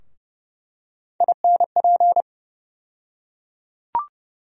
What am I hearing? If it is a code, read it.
Morse code, heard as a keyed tone, about 32 wpm, 700 Hz. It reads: INP
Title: USA Call Sign Postfixes - 3 Letters 30wpm